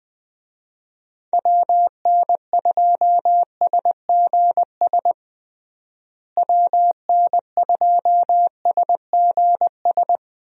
Morse code, heard as a keyed tone, about 20 wpm, 700 Hz. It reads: WN2SGS WN2SGS
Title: USA Call Signs - 6 Characters - Repeated 3x - 20wpm